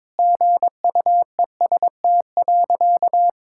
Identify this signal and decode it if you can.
Morse code, heard as a keyed tone, about 22 wpm, 700 Hz. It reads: GUEST.
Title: Sentences from Top 1600 Words 22wpm